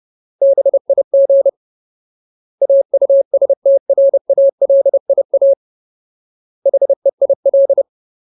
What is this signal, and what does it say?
Morse code, heard as a keyed tone, about 30 wpm, 550 Hz. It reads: BIG AUSTRALIA HEIL
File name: Ultimate Word List / 30WPM Ultimate